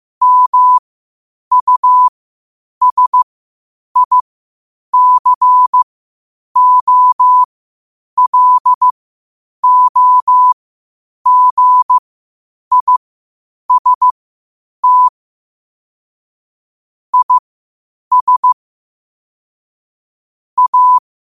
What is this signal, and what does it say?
Morse code, heard as a keyed tone, about 15 wpm, spaced as for 8 wpm, 1000 Hz. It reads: MUSICOLOGIST IS A
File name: qotdfiles / 8wpm